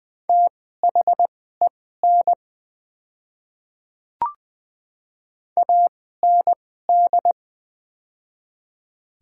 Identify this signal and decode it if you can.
Morse code, heard as a keyed tone, about 20 wpm, spaced as for 14 wpm, 690 Hz. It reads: THEN AND
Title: 2x Character Spacing - Top 100 Words - 20wpm